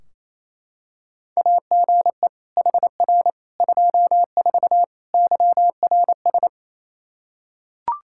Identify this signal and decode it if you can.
Morse code, heard as a keyed tone, about 28 wpm, 700 Hz. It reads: AGE HR 24 YRS
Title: Practice QSO - Exchange 3 - More Info - 28wpm